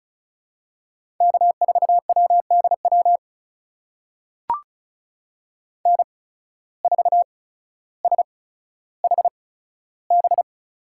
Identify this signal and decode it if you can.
Morse code, heard as a keyed tone, about 36 wpm, spaced as for 15 wpm, 700 Hz. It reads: K4WDW N4SHB
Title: USA Call Signs - 5 Characters - 8x Character Spacing - 35wpm